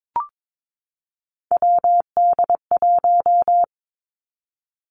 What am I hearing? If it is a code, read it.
Morse code, heard as a keyed tone, about 22 wpm, 700 Hz. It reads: WD1
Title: Three Challenging Letter-Number Combinations 22wpm